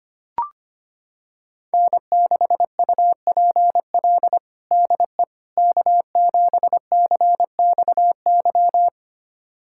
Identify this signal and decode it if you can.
Morse code, heard as a keyed tone, about 25 wpm, 700 Hz. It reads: N6UPL DE K7CXY
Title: USA Callsign DE Callsign - 5 Characters - 25wpm